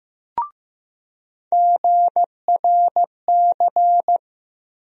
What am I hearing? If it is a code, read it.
Morse code, heard as a keyed tone, about 16 wpm, 700 Hz. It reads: GRC
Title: USA Call Sign Postfixes - 3 Letters 15wpm